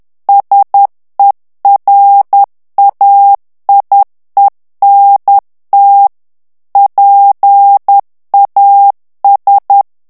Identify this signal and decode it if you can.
Morse code, heard as a keyed tone, about 10 wpm, 800 Hz. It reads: SERAIENT PAS
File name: FRANCAIS TEXTE CLAIR 2 VITESSE 10 MOTS MINUTES